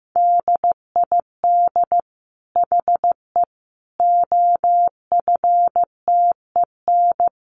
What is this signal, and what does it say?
Morse code, heard as a keyed tone, about 14 wpm, 700 Hz. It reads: DID HE OFTEN
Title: Sentences from Top 1000 Words 15wpm